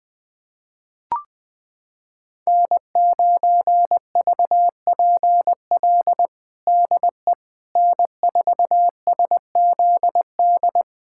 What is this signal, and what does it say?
Morse code, heard as a keyed tone, about 21 wpm, 700 Hz. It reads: N9VPL DE N4SZD
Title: USA Callsign DE Callsign - 5 Characters - 20wpm